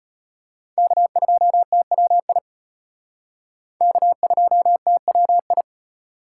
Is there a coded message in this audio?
Yes: Morse code from 0.8 to 5.6 s, heard as a keyed tone, about 32 wpm, 700 Hz. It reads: K2TWI K2TWI